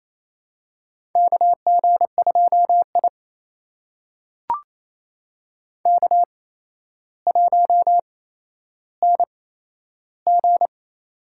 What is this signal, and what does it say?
Morse code, heard as a keyed tone, about 28 wpm, spaced as for 12 wpm, 700 Hz. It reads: KG2I K1NG
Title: USA Call Signs - 4 Characters - 8x Character Spacing - 28wpm